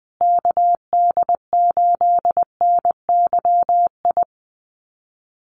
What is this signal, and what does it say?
Morse code, heard as a keyed tone, about 20 wpm, 700 Hz. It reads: KD8NYI
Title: USA Call Signs Repeated 3x 20wpm